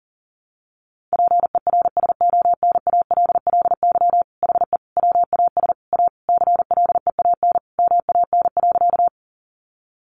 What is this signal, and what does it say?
Morse code, heard as a keyed tone, about 40 wpm, 700 Hz. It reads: PERSONALLY HE WAS A CLEAN MAN.
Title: Sentences from Top 1400 Words 40wpm